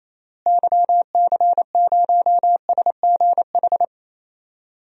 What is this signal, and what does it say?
Morse code, heard as a keyed tone, about 28 wpm, 700 Hz. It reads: YC0SGH
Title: DXCC-100 Call Signs - 6 Characters - Repeated 3x - 28wpm